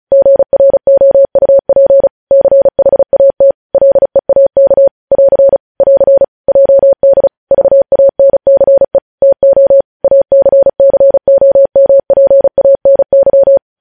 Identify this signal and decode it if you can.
Morse code, heard as a keyed tone, about 36 wpm, 550 Hz. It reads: GROUP CHAT LEAK <AR> <AR> JD VANCE TO ACCOMPANY
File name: podcast / News headlines-1x-35x35